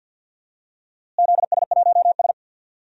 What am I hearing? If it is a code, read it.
Morse code, heard as a keyed tone, about 52 wpm, 700 Hz. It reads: ZS1S
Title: DXCC-100 Call Signs - 4 Characters - Repeated 2x - 50wpm